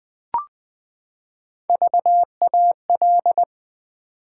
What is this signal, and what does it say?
Morse code, heard as a keyed tone, about 20 wpm, 700 Hz. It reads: VAL
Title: Build A Word - 3 Letters At A Time - 6-Letter Words - 20wpm